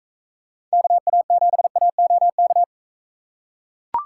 Morse code, heard as a keyed tone, about 43 wpm, 700 Hz. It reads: KA7AOK